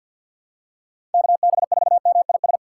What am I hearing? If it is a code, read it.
Morse code, heard as a keyed tone, about 50 wpm, 700 Hz. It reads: KB4MIS
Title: USA Call Signs - 6 Characters - 4x Character Spacing - 50wpm